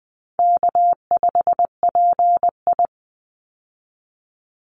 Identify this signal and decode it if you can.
Morse code, heard as a keyed tone, about 20 wpm, 700 Hz. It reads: K5PI
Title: Easy Call Signs Repeated 3x - 20wpm